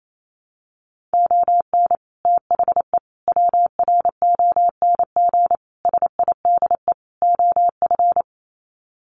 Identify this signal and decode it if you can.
Morse code, heard as a keyed tone, about 28 wpm, 700 Hz. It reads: ON THE WRONG SIDE OF
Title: Sets of 5 Words Encore 28wpm